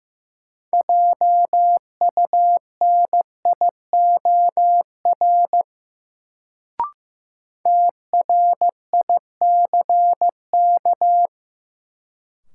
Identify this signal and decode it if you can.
Morse code, heard as a keyed tone, about 15 wpm, 700 Hz. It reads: JUNIOR TRICK
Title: Sentences from Top 1500 Words - New Word Drill Alt - 15wpm